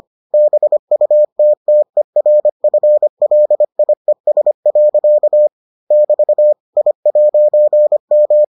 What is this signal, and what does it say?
Morse code, heard as a keyed tone, about 25 wpm, 600 Hz. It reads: BUTTERFLIES. <BT> I'M